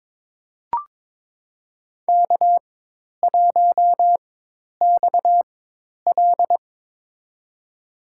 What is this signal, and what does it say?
Morse code, heard as a keyed tone, about 22 wpm, spaced as for 10 wpm, 700 Hz. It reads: K1XL
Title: USA Call Signs - 4 Characters - 4x Character Spacing - 22wpm